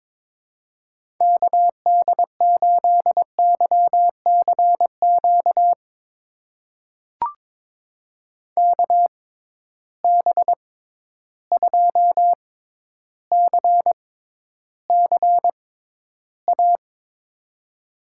Morse code, heard as a keyed tone, about 22 wpm, spaced as for 11 wpm, 700 Hz. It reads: KD8YCQ KB2CCA
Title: USA Call Signs - 6 Characters - 6x Character Spacing - 22wpm